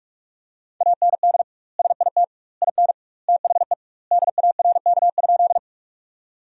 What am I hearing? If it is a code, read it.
Morse code, heard as a keyed tone, about 42 wpm, 700 Hz. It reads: AND SIT IN THE DARK?
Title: 2x Word-Spacing - Sentences from Top 500 Words - 45wpm